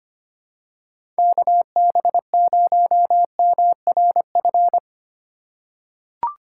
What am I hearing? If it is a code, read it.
Morse code, heard as a keyed tone, about 25 wpm, 700 Hz. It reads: KB0MRF
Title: USA Call Signs - 6 Characters - 3x Character Spacing - 25wpm